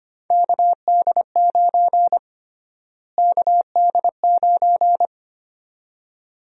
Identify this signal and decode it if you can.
Morse code, heard as a keyed tone, about 25 wpm, 700 Hz. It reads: KD9 KD9
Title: USA Call Sign Prefixes - Repeated 2x - 25wpm